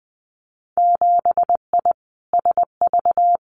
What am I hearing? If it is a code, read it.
Morse code, heard as a keyed tone, about 20 wpm, 700 Hz. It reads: 7I SV
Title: Winter Field Day Exchanges Encore 20wpm